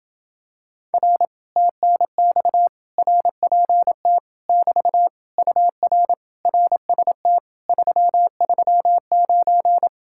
Morse code, heard as a keyed tone, about 27 wpm, 700 Hz. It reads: R TNX RPT <BT> UR RST 339